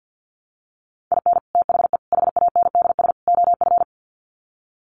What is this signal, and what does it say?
Morse code, heard as a keyed tone, about 50 wpm, 700 Hz. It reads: IN THE HANDS OF
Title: Sets of 4 Words 50wpm